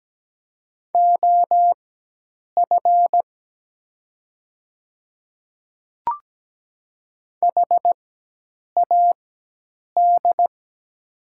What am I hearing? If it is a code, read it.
Morse code, heard as a keyed tone, about 17 wpm, spaced as for 8 wpm, 700 Hz. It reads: OF HAD
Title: 4x Character Spacing - Top 200 Words - 17wpm